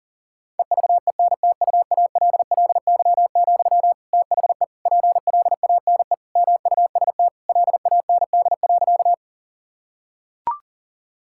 Morse code, heard as a keyed tone, about 40 wpm, 700 Hz. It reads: EVENTUALLY, THE PLANE MUST LAND.
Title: Sentences from Top 1000 Words - Review Part 6 - 40wpm